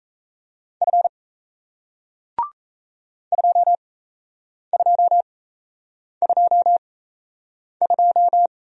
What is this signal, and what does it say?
Morse code, heard as a keyed tone, about 38 wpm, spaced as for 8 wpm, 700 Hz. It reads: F 2222